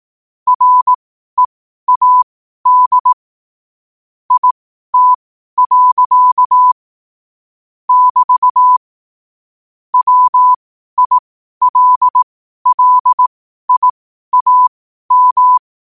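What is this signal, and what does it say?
Morse code, heard as a keyed tone, about 18 wpm, spaced as for 12 wpm, 1000 Hz. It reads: READ IT. <BT> WILLIAM
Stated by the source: Morse at 12 wpm.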